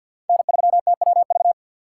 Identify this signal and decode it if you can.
Morse code, heard as a keyed tone, about 51 wpm, 700 Hz. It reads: N3TWV
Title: USA Call Signs - 3x Character Spacing - 50wpm